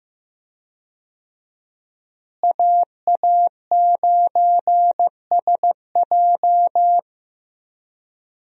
Morse code, heard as a keyed tone, about 15 wpm, 700 Hz. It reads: AA9SJ
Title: USA Call Signs - 5 Characters - Repeated 3x - 15wpm